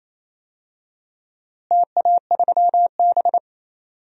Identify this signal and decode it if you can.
Morse code, heard as a keyed tone, about 28 wpm, 700 Hz. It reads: TA3B
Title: DXCC-100 Call Signs - 4 Characters - Repeated 2x - 28wpm